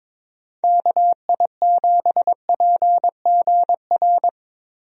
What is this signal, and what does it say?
Morse code, heard as a keyed tone, about 22 wpm, 700 Hz. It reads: KI7PGR
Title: USA Call Signs - 6 Characters - Repeated 3x - 22wpm